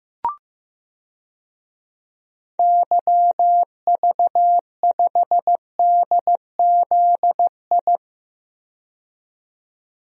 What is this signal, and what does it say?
Morse code, heard as a keyed tone, about 15 wpm, 700 Hz. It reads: YV5DZI
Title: DXCC-100 Call Signs - 6 Characters - 15wpm